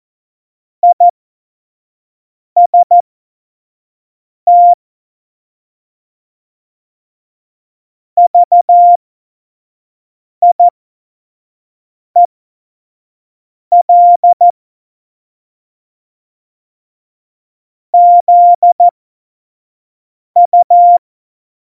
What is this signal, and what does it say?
Morse code, heard as a keyed tone, about 14 wpm, spaced as for 5 wpm, 700 Hz. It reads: IST VIEL ZU
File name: Morsekurs / Witze / Witz 02. Juli 2019, 25 F70